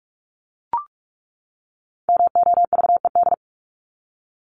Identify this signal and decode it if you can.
Morse code, heard as a keyed tone, about 44 wpm, 700 Hz. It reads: MOVED
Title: ED Suffix - Top 100 Words with Base Words 45wpm